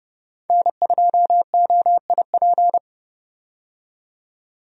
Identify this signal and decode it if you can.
Morse code, heard as a keyed tone, about 29 wpm, 700 Hz. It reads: N2OIP
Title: USA Call Signs - 5 Characters - 30wpm